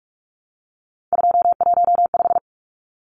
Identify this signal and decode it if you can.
Morse code, heard as a keyed tone, about 44 wpm, 700 Hz. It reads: 215
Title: Numbers - 3-Digit 45wpm